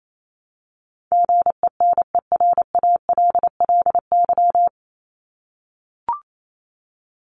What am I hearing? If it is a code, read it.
Morse code, heard as a keyed tone, about 28 wpm, 700 Hz. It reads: GENERALLY
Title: LY Suffix - Top 100 Words with Base Words 28wpm